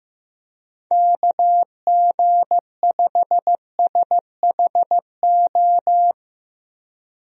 Morse code, heard as a keyed tone, about 15 wpm, 700 Hz. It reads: KG5SHO
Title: USA Call Signs - 6 Characters - Repeated 2x - 15wpm